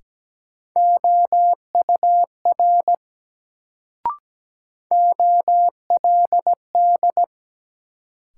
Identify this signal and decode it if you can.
Morse code, heard as a keyed tone, about 17 wpm, 700 Hz. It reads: OUR OLD
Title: Sentences from Top 200 Words - New Word Drill - 17wpm